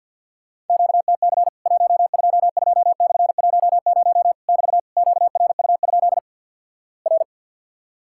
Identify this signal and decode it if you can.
Morse code, heard as a keyed tone, about 50 wpm, 700 Hz. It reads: QTC 122/10 <BT> QRU? R
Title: Worked All Europe DX Contest - QTC Exchange with Repeats - 50wpm